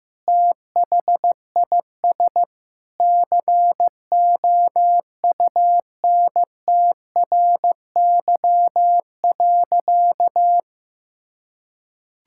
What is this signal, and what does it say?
Morse code, heard as a keyed tone, about 16 wpm, 700 Hz. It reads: THIS COUNTRY.
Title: Sentences from Top 1000 Words - Review Part 5 - 15wpm